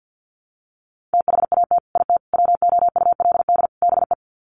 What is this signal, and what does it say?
Morse code, heard as a keyed tone, about 48 wpm, 700 Hz. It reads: THAT IT WOULD BE
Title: Sets of 4 Words 50wpm